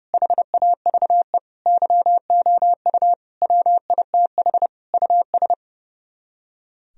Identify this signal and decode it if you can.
Morse code, heard as a keyed tone, about 38 wpm, 700 Hz. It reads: HAVE YOU WITH US